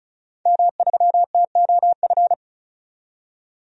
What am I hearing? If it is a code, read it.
Morse code, heard as a keyed tone, about 35 wpm, 700 Hz. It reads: M3TOF
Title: DXCC-100 Call Signs - 5 Characters - 35wpm